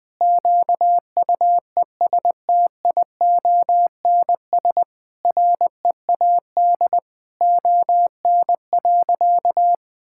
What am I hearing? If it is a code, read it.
Morse code, heard as a keyed tone, about 19 wpm, 700 Hz. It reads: QUESTIONS READ ON.
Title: Sentences with 9-Word Phrases - 20wpm